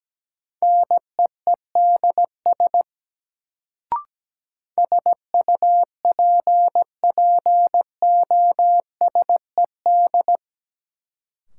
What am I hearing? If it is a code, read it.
Morse code, heard as a keyed tone, about 17 wpm, 700 Hz. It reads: NEEDS SUPPOSED
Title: Sets of 3 Words - New Word Drill - 17wpm